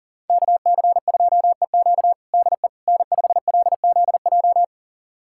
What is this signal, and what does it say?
Morse code, heard as a keyed tone, about 40 wpm, 690 Hz. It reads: KC2EQ DE N5LZJ
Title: USA Callsign DE Callsign - 5 Characters - 40wpm